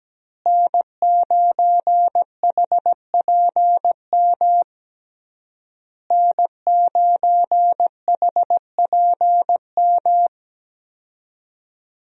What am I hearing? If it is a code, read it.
Morse code, heard as a keyed tone, about 17 wpm, 700 Hz. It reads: N9HPM N9HPM
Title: USA Call Signs - 5 Characters - Repeated 3x - 17wpm